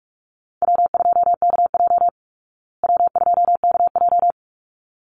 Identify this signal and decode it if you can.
Morse code, heard as a keyed tone, about 44 wpm, 700 Hz. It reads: W2KJ W2KJ
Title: Call Signs Repeated 3x - 45wpm